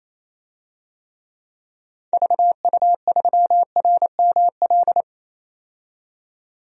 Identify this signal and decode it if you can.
Morse code, heard as a keyed tone, about 28 wpm, 700 Hz. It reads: VU3RML
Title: DXCC-100 Call Signs - 6 Characters - 28wpm